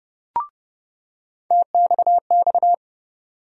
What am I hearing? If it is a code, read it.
Morse code, heard as a keyed tone, about 30 wpm, 700 Hz. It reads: TXX
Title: USA Call Sign Postfixes - 3 Letters 30wpm